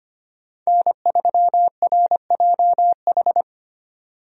Morse code, heard as a keyed tone, about 25 wpm, 700 Hz. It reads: N3RJH
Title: USA Call Signs - 5 Characters - 25wpm